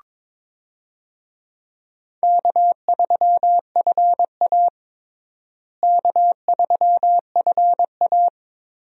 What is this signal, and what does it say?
Morse code, heard as a keyed tone, about 22 wpm, 700 Hz. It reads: K3FA K3FA
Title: USA Call Signs - 4 Characters - Repeated 3x - 22wpm